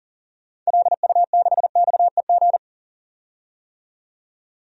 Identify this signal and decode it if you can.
Morse code, heard as a keyed tone, about 40 wpm, 700 Hz. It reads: LU6XEG
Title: DXCC-100 Call Signs - 6 Characters - 40wpm